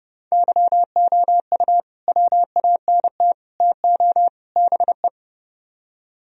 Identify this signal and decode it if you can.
Morse code, heard as a keyed tone, about 39 wpm, 700 Hz. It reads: YOU WANT TO BE